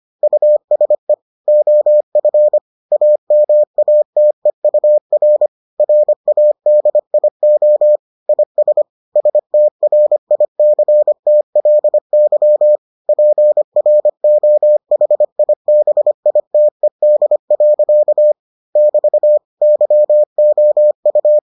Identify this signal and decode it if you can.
Morse code, heard as a keyed tone, about 25 wpm, 600 Hz. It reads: USE OF AMATEUR RADIO IS STRICTLY PROHIBITED. <BT> YOU